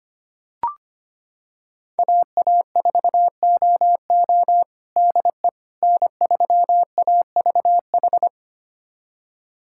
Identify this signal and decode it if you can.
Morse code, heard as a keyed tone, about 25 wpm, 700 Hz. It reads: AA4OO DE N3AVH
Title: USA Callsign DE Callsign - 5 Characters - 25wpm